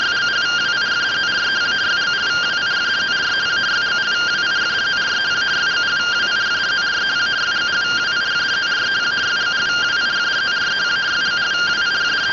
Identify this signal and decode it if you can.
Two-tone teleprinter signal, about 170 Hz shift.